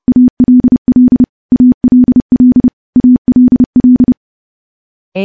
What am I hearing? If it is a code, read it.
Morse code, heard as a keyed tone, about 30 wpm, 250 Hz. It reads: ALL ALL ALL
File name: morse / morse-training-30WPM